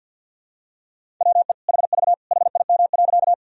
Morse code, heard as a keyed tone, about 48 wpm, 700 Hz. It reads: WE HV HIM.
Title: Sentences from Top 100 Words and Abbreviations 50wpm